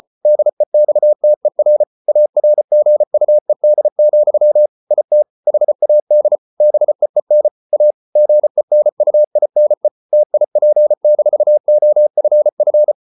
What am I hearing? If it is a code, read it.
Morse code, heard as a keyed tone, about 34 wpm, 600 Hz. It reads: DEXTER ARGUED, IT HAD BEEN A GENUINE TIP-OFF